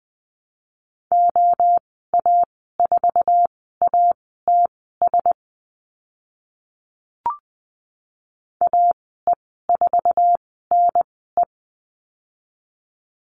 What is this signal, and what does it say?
Morse code, heard as a keyed tone, about 20 wpm, spaced as for 14 wpm, 700 Hz. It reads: OA4ATS AE4NE
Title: 076 Lesson 13g - Letter L - Call Signs 20wpm